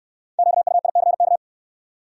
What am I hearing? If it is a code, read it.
Morse code, heard as a keyed tone, about 69 wpm, 700 Hz. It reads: QUICK